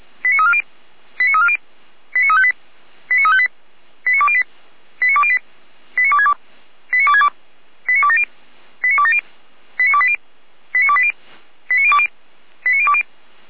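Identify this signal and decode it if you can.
Five-tone selective call (ZVEI), heard as quick runs of five tones, tones of about 70 ms. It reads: 89340 89340 89358 89358 89208 89208 89262 89262 89270 89270 89280 89280 89020 89020
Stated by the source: ZVEI five-tone selective call received by SDR.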